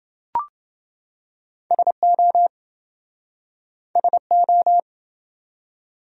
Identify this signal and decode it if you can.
Morse code, heard as a keyed tone, about 27 wpm, 700 Hz. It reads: SO SO